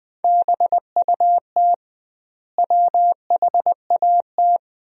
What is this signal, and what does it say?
Morse code, heard as a keyed tone, about 20 wpm, 700 Hz. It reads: BUT WHAT
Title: 2x Word-Spacing - Sentences from Top 100 Words - 20wpm